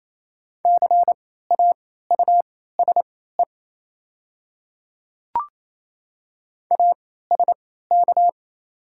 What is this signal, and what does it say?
Morse code, heard as a keyed tone, about 28 wpm, spaced as for 16 wpm, 700 Hz. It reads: CAUSE ASK
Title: 3x Character Spacing - Top 500 Words - 28wpm